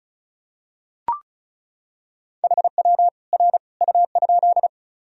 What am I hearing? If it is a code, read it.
Morse code, heard as a keyed tone, about 34 wpm, 700 Hz. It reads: HW R U?
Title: Sentences from Top 100 Words and Abbreviations 35wpm